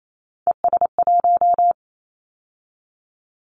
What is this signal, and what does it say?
Morse code, heard as a keyed tone, about 28 wpm, 700 Hz. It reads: ES1
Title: Callsign Prefixes for Top 50 DXCC - 28wpm